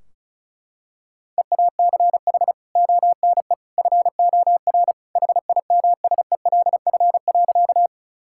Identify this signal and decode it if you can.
Morse code, heard as a keyed tone, about 34 wpm, 700 Hz. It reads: EACH ONE FOR HIMSELF.
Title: Sentences from Top 1600 Words 35wpm